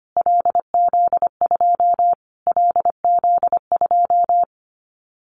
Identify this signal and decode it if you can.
Morse code, heard as a keyed tone, about 25 wpm, 700 Hz. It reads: LZ2 LZ2
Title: Callsign Prefixes for Top 25 DXCC - Repeated 3x - 25wpm